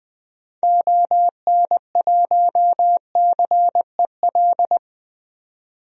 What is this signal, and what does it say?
Morse code, heard as a keyed tone, about 20 wpm, 700 Hz. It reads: ON1CEL
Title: DXCC-100 Call Signs - 6 Characters - 20wpm